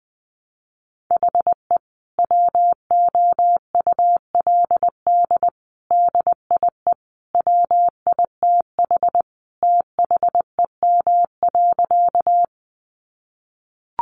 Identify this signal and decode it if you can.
Morse code, heard as a keyed tone, about 20 wpm, 700 Hz. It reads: HE WOULD DIE WITH THEM.
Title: Sentences from Top 1300 Words 20wpm